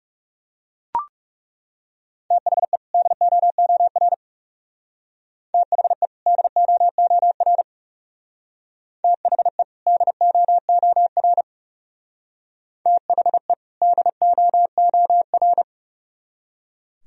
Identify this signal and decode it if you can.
Morse code, heard as a keyed tone, about 40 wpm, 700 Hz. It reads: THE DOOR THE DOOR THE DOOR THE DOOR